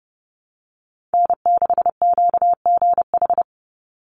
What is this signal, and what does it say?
Morse code, heard as a keyed tone, about 30 wpm, 700 Hz. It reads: N6QGH
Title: USA Call Signs 30wpm